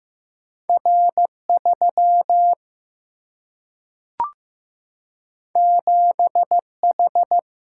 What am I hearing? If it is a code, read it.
Morse code, heard as a keyed tone, about 15 wpm, 700 Hz. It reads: R3 7H
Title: Single-Letter Single-Number 15wpm